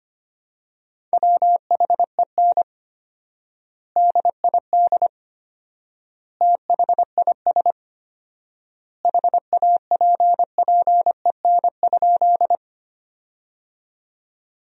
Morse code, heard as a keyed tone, about 25 wpm, 700 Hz. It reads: WHEN DID THIS HAPPEN?
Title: 4x Word-Spacing - Sentences from Top 400 Words - 25wpm